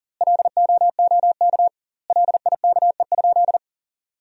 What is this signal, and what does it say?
Morse code, heard as a keyed tone, about 40 wpm, 700 Hz. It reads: LOOK LIKE?
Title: 2x Word-Spacing - Sentences from Top 100 Words - 40wpm